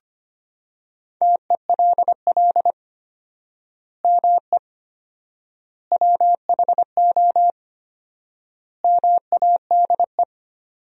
Morse code, heard as a keyed tone, about 25 wpm, 700 Hz. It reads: TELL ME WHO MADE
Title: 4x Word-Spacing - Sentences from Top 200 Words - 25wpm